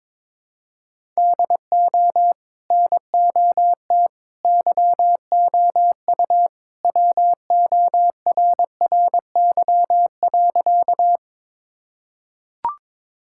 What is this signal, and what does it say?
Morse code, heard as a keyed tone, about 22 wpm, 700 Hz. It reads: DO NOT YOU WORRY.
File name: practice / 22wpm / Sentences with 3-Word Phrases - 22wpm